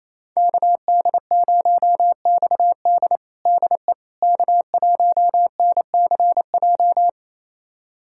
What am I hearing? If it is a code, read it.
Morse code, heard as a keyed tone, about 28 wpm, 700 Hz. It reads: KD0XD DE K1NCJ
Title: USA Callsign DE Callsign - 5 Characters - 28wpm